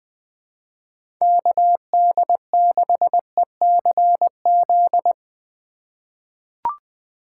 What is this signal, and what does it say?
Morse code, heard as a keyed tone, about 20 wpm, 700 Hz. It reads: KD6ECZ
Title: USA Call Signs - 3x Character Spacing - 20wpm